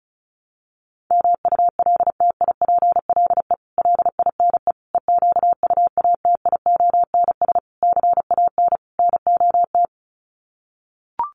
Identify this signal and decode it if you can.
Morse code, heard as a keyed tone, about 34 wpm, 700 Hz. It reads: MULTIPLE LINE EQUATIONS CAN NOT
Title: Sets of 5 Words 35wpm